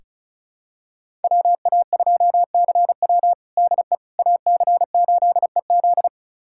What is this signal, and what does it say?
Morse code, heard as a keyed tone, about 36 wpm, 700 Hz. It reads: WA2CW DE AC8EZ
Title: USA Callsign DE Callsign - 5 Characters - 35wpm